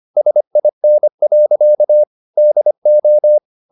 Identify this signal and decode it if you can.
Morse code, heard as a keyed tone, about 25 wpm, 600 Hz. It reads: SIN. DO